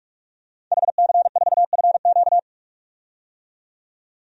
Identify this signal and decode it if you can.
Morse code, heard as a keyed tone, about 43 wpm, 700 Hz. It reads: HC4FQ
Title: DXCC-100 Call Signs - 5 Characters - 45wpm